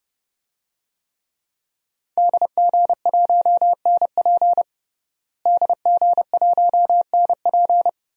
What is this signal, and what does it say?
Morse code, heard as a keyed tone, about 30 wpm, 700 Hz. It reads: DG1NP DG1NP
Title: DXCC-100 Call Signs - 5 Characters - Repeated 2x - 30wpm